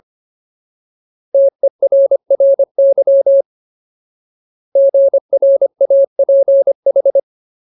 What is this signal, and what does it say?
Morse code, heard as a keyed tone, about 25 wpm, 550 Hz. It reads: TERRY GRAPH
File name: Ultimate Word List / 25WPM Ultimate